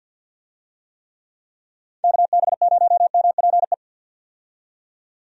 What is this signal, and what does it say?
Morse code, heard as a keyed tone, about 46 wpm, 700 Hz. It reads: KB0MPE
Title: USA Call Signs - 6 Characters - 50wpm